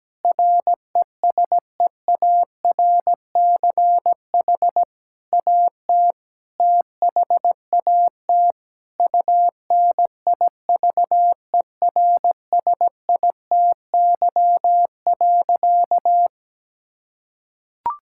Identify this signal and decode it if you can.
Morse code, heard as a keyed tone, about 17 wpm, 700 Hz. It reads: RESEARCH AT THAT UNIVERSITY.
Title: Sentences from Top 1000 Words - Review Part 6 - 17wpm